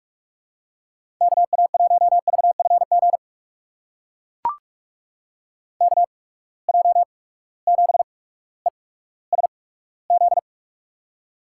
Morse code, heard as a keyed tone, about 42 wpm, spaced as for 19 wpm, 700 Hz. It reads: KA1VFG KJ7ESZ
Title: USA Call Signs - 6 Characters - 8x Character Spacing - 45wpm